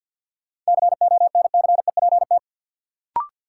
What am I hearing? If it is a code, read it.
Morse code, heard as a keyed tone, about 50 wpm, 700 Hz. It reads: CONCEPT
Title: Sentences from Top 1100 Words - New Word Drill Alt - 50wpm